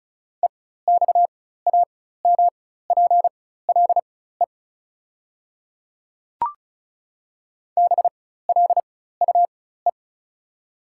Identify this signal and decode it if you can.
Morse code, heard as a keyed tone, about 36 wpm, spaced as for 16 wpm, 690 Hz. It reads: EXAMPLE BLUE
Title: 4x Character Spacing - Top 500 Words - 35wpm